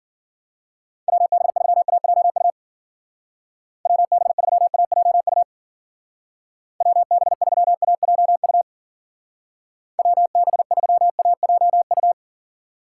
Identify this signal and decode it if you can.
Morse code, heard as a keyed tone, about 53 wpm, 700 Hz. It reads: WB3AJU WB3AJU WB3AJU WB3AJU